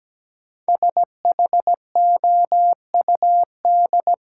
Morse code, heard as a keyed tone, about 17 wpm, 700 Hz. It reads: SHOUD
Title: Sentences from Top 1000 Words - Review Part 4 - New Word Drill - 17wpm